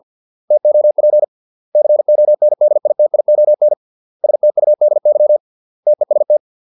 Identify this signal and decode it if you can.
Morse code, heard as a keyed tone, about 50 wpm, 600 Hz. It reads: TOP CONDITION STUDY TEST